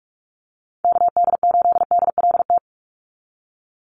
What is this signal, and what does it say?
Morse code, heard as a keyed tone, about 44 wpm, 700 Hz. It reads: KD8DLT
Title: USA Call Signs 45wpm